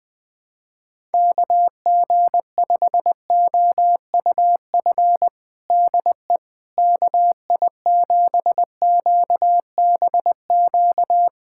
Morse code, heard as a keyed tone, about 21 wpm, 700 Hz. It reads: KG5OUF DE KI7QBQ
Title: USA Callsign DE Callsign 20wpm